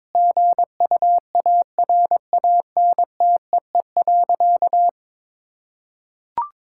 Morse code, heard as a keyed tone, about 23 wpm, 700 Hz. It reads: GUARANTEE.
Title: Sentences from Top 1600 Words 22wpm